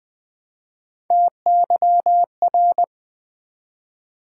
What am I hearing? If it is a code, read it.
Morse code, heard as a keyed tone, about 20 wpm, 700 Hz. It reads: TYR
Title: Build A Word - 1 Letter At A Time - 4-Letter Words - 20wpm